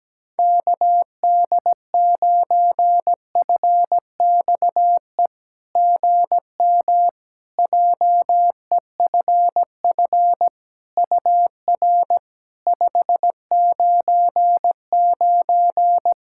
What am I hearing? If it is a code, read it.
Morse code, heard as a keyed tone, about 17 wpm, 700 Hz. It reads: KD9FXE GM JEFF UR 599
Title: SOTA QSO Practice 17wpm